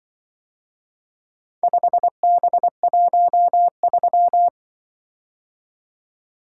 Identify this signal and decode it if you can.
Morse code, heard as a keyed tone, about 24 wpm, 700 Hz. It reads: 5B13
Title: Speed-Racing - Four Letter-Number Combinations - 20wpm